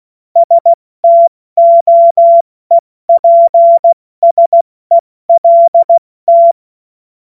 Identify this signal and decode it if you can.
Morse code, heard as a keyed tone, about 16 wpm, 680 Hz. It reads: STOEPSELT
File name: Morsekurs / Witze / Witz 21. Dezember 2021, 70 F80